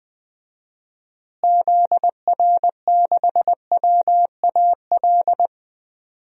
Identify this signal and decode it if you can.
Morse code, heard as a keyed tone, about 21 wpm, 700 Hz. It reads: ZR6WAL
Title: DXCC-100 Call Signs - 6 Characters - Repeated 3x - 20wpm